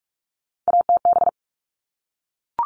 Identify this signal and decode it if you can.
Morse code, heard as a keyed tone, about 44 wpm, 700 Hz. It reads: ATB
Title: All Cut Numbers - 3-Digit 45wpm